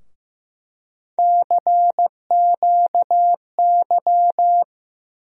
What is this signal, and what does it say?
Morse code, heard as a keyed tone, about 15 wpm, 700 Hz. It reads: CQY
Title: USA Call Sign Postfixes - 3 Letters 15wpm